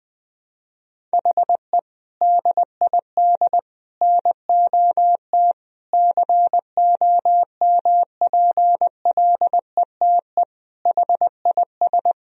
Morse code, heard as a keyed tone, about 20 wpm, 700 Hz. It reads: HE DID NOT COMPLETE HIS
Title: Sentences from Top 1000 Words - Review Part 1 - 20wpm